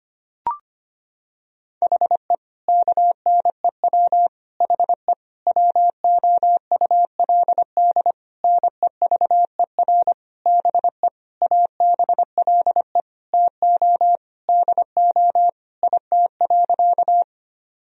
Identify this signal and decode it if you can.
Morse code, heard as a keyed tone, about 25 wpm, 700 Hz. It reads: HE KNEW HE WOULD NEVER BE ABLE TO DO IT.
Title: Sentences from Top 1000 Words - Review Part 2 - 25wpm